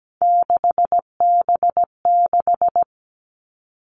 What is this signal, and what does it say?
Morse code, heard as a keyed tone, about 17 wpm, 700 Hz. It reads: 6B6
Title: All Cut Numbers - 3-Digit 17wpm